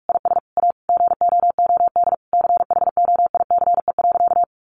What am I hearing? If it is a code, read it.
Morse code, heard as a keyed tone, about 44 wpm, 700 Hz. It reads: IS A GOOD CHOICE.
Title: Sentences from Top 900 Words 45wpm